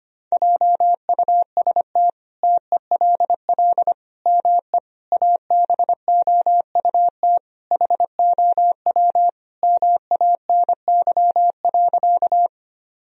Morse code, heard as a keyed tone, about 25 wpm, 700 Hz. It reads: JUST TELL ME ABOUT HOW MANY.
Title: Sentences from Top 1000 Words - Review Part 1 - 25wpm